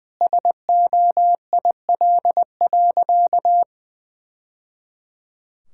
Morse code, heard as a keyed tone, about 20 wpm, 700 Hz. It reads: SOIL.
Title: Sentences from Top 1000 Words - Review Part 5 - 20wpm